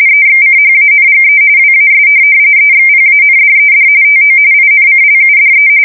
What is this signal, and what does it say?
RTTY, about 45.45 baud. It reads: EXCHANGE WAS EXPANDED TO INCLUDE
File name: fsk170 / teletype-RTTY